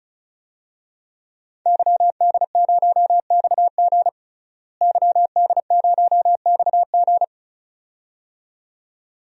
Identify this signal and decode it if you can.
Morse code, heard as a keyed tone, about 34 wpm, 700 Hz. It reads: YD0XG YD0XG
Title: DXCC-100 Call Signs - 5 Characters - Repeated 2x - 35wpm